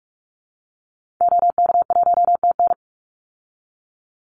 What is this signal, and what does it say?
Morse code, heard as a keyed tone, about 44 wpm, 700 Hz. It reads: OK1TN
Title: Call Signs Repeated 2x - 45wpm